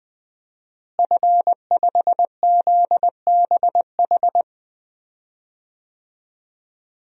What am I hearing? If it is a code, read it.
Morse code, heard as a keyed tone, about 20 wpm, 700 Hz. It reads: F5ZBH
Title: DXCC-100 Call Signs - 5 Characters - Repeated 2x - 20wpm